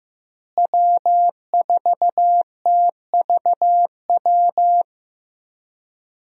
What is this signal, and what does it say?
Morse code, heard as a keyed tone, about 16 wpm, 700 Hz. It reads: W4TVW
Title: USA Call Signs - 5 Characters - Repeated 3x - 15wpm